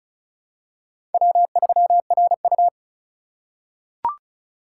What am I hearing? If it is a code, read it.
Morse code, heard as a keyed tone, about 36 wpm, 700 Hz. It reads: W3RU
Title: USA Call Signs - 4 Characters - 3x Character Spacing - 35wpm